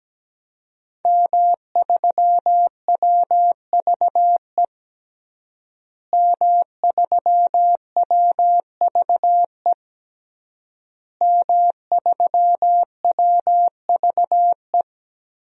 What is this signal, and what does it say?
Morse code, heard as a keyed tone, about 17 wpm, 700 Hz. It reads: M3WVE M3WVE M3WVE
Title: DXCC-100 Call Signs - 5 Characters - Repeated 3x - 17wpm